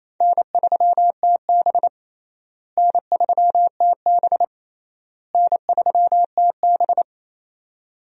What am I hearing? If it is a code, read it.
Morse code, heard as a keyed tone, about 28 wpm, 700 Hz. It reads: N3TB N3TB N3TB
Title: USA Call Signs - 4 Characters - Repeated 3x - 28wpm